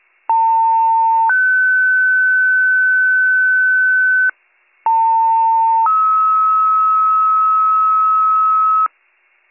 Two-tone page, heard as a one-second tone, then a three-second tone, about 900 then 1500 Hz, about 900 then 1290 Hz, with noise. It is heard from 0.3 to 8.9 s.